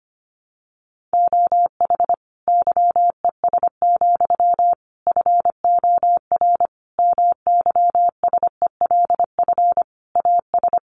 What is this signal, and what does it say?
Morse code, heard as a keyed tone, about 25 wpm, 700 Hz. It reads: OH YES, FOR MYSELF AS
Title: Sentences from Top 1300 Words 25wpm